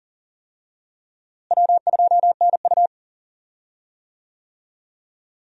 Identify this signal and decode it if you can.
Morse code, heard as a keyed tone, about 40 wpm, 700 Hz. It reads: W2NU
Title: USA Call Signs - 4 Characters - Repeated 3x - 40wpm